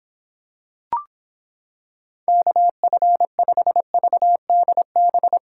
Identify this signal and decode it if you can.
Morse code, heard as a keyed tone, about 26 wpm, 700 Hz. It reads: KF5VDB